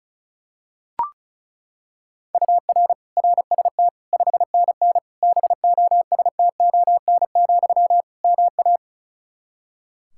Morse code, heard as a keyed tone, about 36 wpm, 700 Hz. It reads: UR RST 5NN BOSTON, MA
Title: Practice QSO - Exchange 1 - Beginning - 35wpm